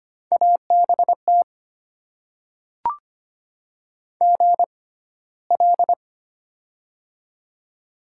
Morse code, heard as a keyed tone, about 25 wpm, spaced as for 15 wpm, 700 Hz. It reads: ABT GL
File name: Top 90 QSO Elements - 6x Character Spacing - 25wpm